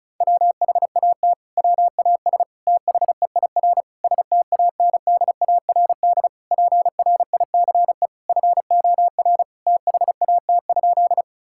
Morse code, heard as a keyed tone, about 36 wpm, 700 Hz. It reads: WHAT WAS THEIR STANDARD PRICE FOR THAT?
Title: Sentences from Top 1000 Words - Review Part 6 - 35wpm